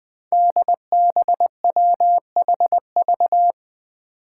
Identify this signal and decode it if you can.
Morse code, heard as a keyed tone, about 20 wpm, 700 Hz. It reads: DBWHV
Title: Five Challenging Letter-Number Combinations 20wpm